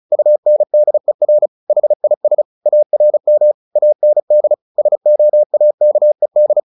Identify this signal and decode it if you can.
Morse code, heard as a keyed tone, about 34 wpm, 600 Hz. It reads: UNDER HIS ARM AND SOAKED
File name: the old man and the sea 2 35wpm